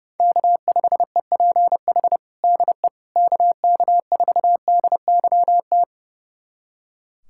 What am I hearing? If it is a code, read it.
Morse code, heard as a keyed tone, about 30 wpm, 700 Hz. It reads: K5EPH DE KK4DYT
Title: USA Callsign DE Callsign 30wpm